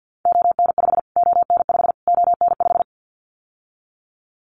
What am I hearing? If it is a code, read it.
Morse code, heard as a keyed tone, about 50 wpm, 700 Hz. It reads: ON5 ON5 ON5
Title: Callsign Prefixes for Top 25 DXCC - Repeated 3x - 50wpm